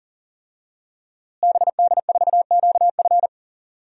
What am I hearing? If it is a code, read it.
Morse code, heard as a keyed tone, about 37 wpm, 700 Hz. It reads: BD4QF
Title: DXCC-100 Call Signs - 5 Characters - Repeated 3x - 40wpm